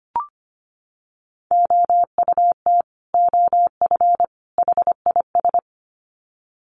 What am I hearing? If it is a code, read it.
Morse code, heard as a keyed tone, about 25 wpm, 700 Hz. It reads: OUT OF HIS
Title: Sets of 3 Words 25wpm